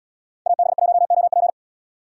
Morse code, heard as a keyed tone, about 77 wpm, 700 Hz. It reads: AB1OP